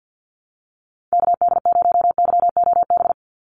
Morse code, heard as a keyed tone, about 50 wpm, 700 Hz. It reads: KD0YOB
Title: NA SOTA Activator Callsigns 50wpm